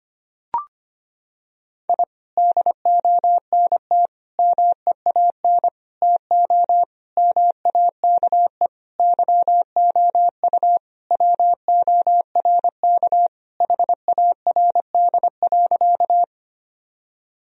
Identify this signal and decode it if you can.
Morse code, heard as a keyed tone, about 25 wpm, 700 Hz. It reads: I DONT MEAN TO MAKE YOU WORK HARD.
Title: Sentences from Top 1000 Words - Review Part 1 - 25wpm